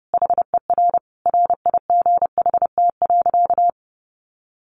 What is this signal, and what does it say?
Morse code, heard as a keyed tone, about 30 wpm, 700 Hz. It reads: HER RIGHT.
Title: Sentences from Top 1300 Words 30wpm